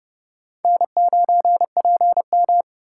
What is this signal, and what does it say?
Morse code, heard as a keyed tone, about 29 wpm, 700 Hz. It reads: N9PM